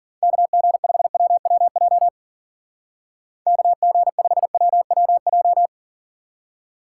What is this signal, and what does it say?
Morse code, heard as a keyed tone, about 47 wpm, 700 Hz. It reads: KG5WWJ KG5WWJ
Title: Speed Racing - USA 2x3 Call Signs 40wpm